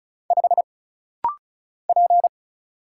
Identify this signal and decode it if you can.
Morse code, heard as a keyed tone, about 34 wpm, 700 Hz. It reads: 5 P
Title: Single Letter-Number - Rapid-Fire 35wpm